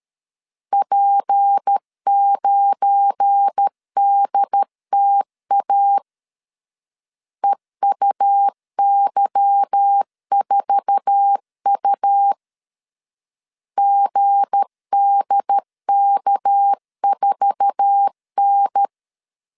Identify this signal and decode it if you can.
Morse code, heard as a keyed tone, about 13 wpm, 800 Hz. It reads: P9DTA EUY4U GDK4N
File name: LEZIONI / 8 YJ45 / 8 YJ4 12wpm 2151